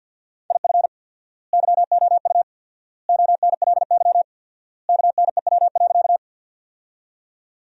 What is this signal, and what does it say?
Morse code, heard as a keyed tone, about 50 wpm, 700 Hz. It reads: IF YOU ONLY KNEW.
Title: 4x Word-Spacing - Sentences from Top 400 Words - 50wpm